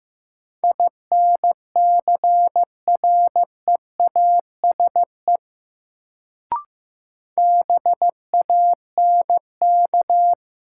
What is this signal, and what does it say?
Morse code, heard as a keyed tone, about 15 wpm, 700 Hz. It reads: INCREASE BANK
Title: Sets of 5 Words - New Word Drill - 15wpm